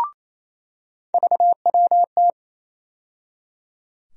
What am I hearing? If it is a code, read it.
Morse code, heard as a keyed tone, about 28 wpm, 700 Hz. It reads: VWT
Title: USA Call Sign Postfixes - 3 Letters 28wpm